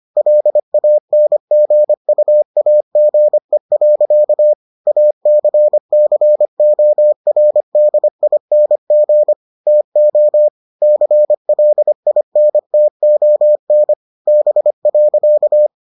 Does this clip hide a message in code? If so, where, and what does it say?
Yes: Morse code from 0.2 to 15.7 s, heard as a keyed tone, about 25 wpm, 600 Hz. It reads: LANGUAGE. ACCORDING TO CLINTON B.